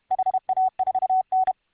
Morse code, heard as a keyed tone, about 32 wpm, 750 Hz. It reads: HA4N